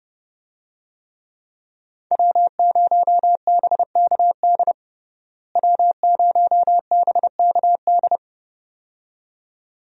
Morse code, heard as a keyed tone, about 32 wpm, 700 Hz. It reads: W0BKD W0BKD
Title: USA Call Signs - 5 Characters - Repeated 2x - 30wpm